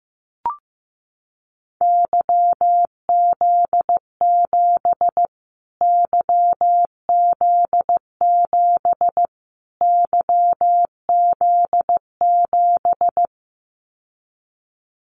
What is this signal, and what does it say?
Morse code, heard as a keyed tone, about 15 wpm, 700 Hz. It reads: YZ7 YZ7 YZ7
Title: Callsign Prefixes for Top 25 DXCC - Repeated 3x - 15wpm